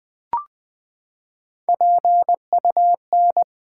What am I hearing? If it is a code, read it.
Morse code, heard as a keyed tone, about 20 wpm, 700 Hz. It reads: PUN
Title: Build A Word - 3 Letters At A Time - 6-Letter Words - 20wpm